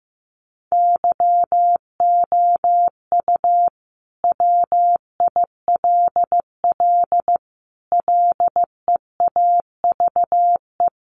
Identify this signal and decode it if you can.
Morse code, heard as a keyed tone, about 15 wpm, 700 Hz. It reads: YOU WILL LEAVE
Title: Sentences from Top 900 Words 15wpm